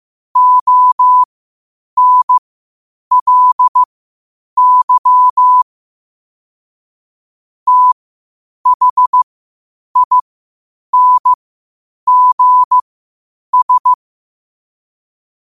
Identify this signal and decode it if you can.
Morse code, heard as a keyed tone, about 15 wpm, spaced as for 8 wpm, 1000 Hz. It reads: ONLY THINGS
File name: qotdfiles / 8wpm